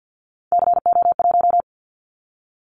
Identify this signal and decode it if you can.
Morse code, heard as a keyed tone, about 50 wpm, 700 Hz. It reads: CO1
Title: Callsign Prefixes for Top 100 DXCC - 50wpm